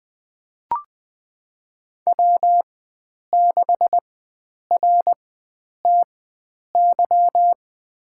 Morse code, heard as a keyed tone, about 21 wpm, spaced as for 9 wpm, 700 Hz. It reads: W6RTY
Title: USA Call Signs - 5 Characters - 4x Character Spacing - 20wpm